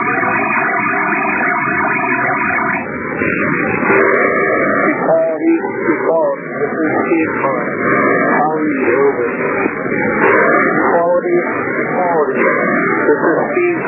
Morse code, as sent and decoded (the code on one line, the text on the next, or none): none